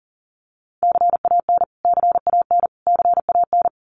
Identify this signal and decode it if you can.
Morse code, heard as a keyed tone, about 40 wpm, 700 Hz. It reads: CAN CAN CAN
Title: Top 500 Words Repeated 40wpm